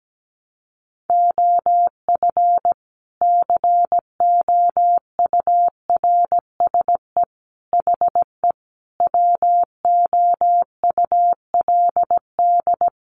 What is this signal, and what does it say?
Morse code, heard as a keyed tone, about 17 wpm, 700 Hz. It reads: OF COURSE HE WOULD
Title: Sentences from Top 1200 Words 17wpm